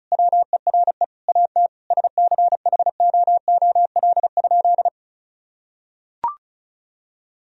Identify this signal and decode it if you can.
Morse code, heard as a keyed tone, about 36 wpm, 700 Hz. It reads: WERE AT SCHOOL?
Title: Sentences from Top 1000 Words - Review Part 1 - 35wpm